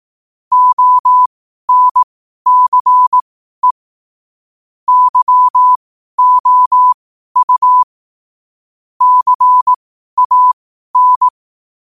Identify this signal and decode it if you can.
Morse code, heard as a keyed tone, about 18 wpm, spaced as for 12 wpm, 1000 Hz. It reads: ONCE YOU CAN
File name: qotdfiles / 12wpm